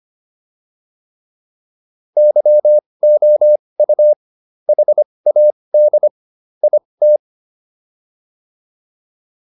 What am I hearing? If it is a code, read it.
Morse code, heard as a keyed tone, about 25 wpm, spaced as for 20 wpm, 600 Hz. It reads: YOU HAD IT